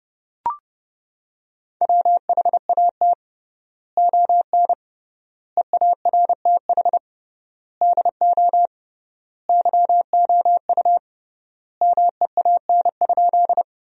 Morse code, heard as a keyed tone, about 32 wpm, 700 Hz. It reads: WHAT ON EARTH DO YOU MEAN?
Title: 3x Word-Spacing - Sentences from Top 200 Words - 30wpm